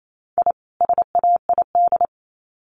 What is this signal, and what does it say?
Morse code, heard as a keyed tone, about 28 wpm, 700 Hz. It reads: I SAID
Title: Sets of 2 Words 28wpm